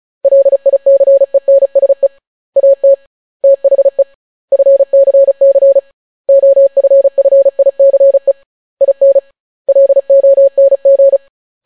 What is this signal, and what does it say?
Morse code, heard as a keyed tone, about 36 wpm, 550 Hz. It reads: LICENSE AT THE FCC OFFICE IN LONG